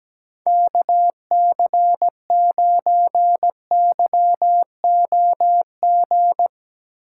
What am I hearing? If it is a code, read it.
Morse code, heard as a keyed tone, about 17 wpm, 700 Hz. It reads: KC9YOG
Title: USA Call Signs - 6 Characters - 6x Character Spacing - 17wpm